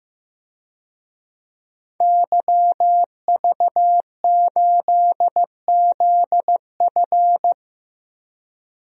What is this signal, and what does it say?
Morse code, heard as a keyed tone, about 15 wpm, 700 Hz. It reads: YV8ZF
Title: DXCC-100 Call Signs - 5 Characters - 15wpm